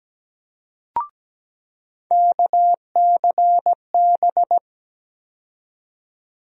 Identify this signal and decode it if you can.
Morse code, heard as a keyed tone, about 17 wpm, 700 Hz. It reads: KCB
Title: USA Call Sign Postfixes - 3 Letters 17wpm